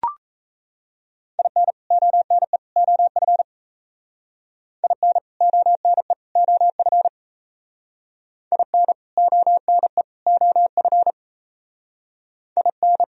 Morse code, heard as a keyed tone, about 35 wpm, 700 Hz. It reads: IN ONE OF IN ONE OF IN ONE OF IN
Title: Speed-Racing - Sets of 3 Words 28wpm